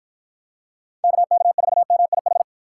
Morse code, heard as a keyed tone, about 50 wpm, 700 Hz. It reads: KK4MIH